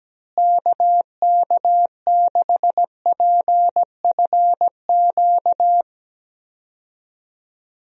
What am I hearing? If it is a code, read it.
Morse code, heard as a keyed tone, about 17 wpm, 700 Hz. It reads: KK6PFQ
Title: USA Call Signs - 6 Characters - Repeated 3x - 17wpm